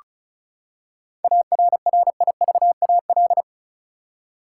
Morse code, heard as a keyed tone, about 34 wpm, 700 Hz. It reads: ARRIVAL
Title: Seven Letter Words - 35wpm